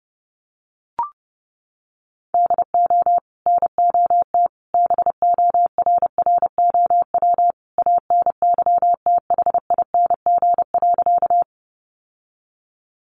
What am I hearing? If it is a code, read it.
Morse code, heard as a keyed tone, about 30 wpm, 700 Hz. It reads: DO NOT BORROW ANYTHING.
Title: Sentences from Top 1500 Words 30wpm